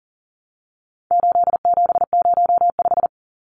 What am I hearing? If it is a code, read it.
Morse code, heard as a keyed tone, about 40 wpm, 700 Hz. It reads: 8705
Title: Numbers - 4-Digit 40wpm